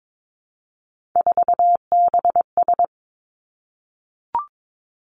Morse code, heard as a keyed tone, about 22 wpm, 700 Hz. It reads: 4BS
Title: Three Challenging Letter-Number Combinations 22wpm